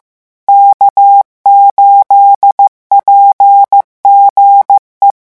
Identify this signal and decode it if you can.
Morse code, heard as a keyed tone, about 15 wpm, 800 Hz. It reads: K8PGE
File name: morse mp3 / revision12-vitesse 14 mots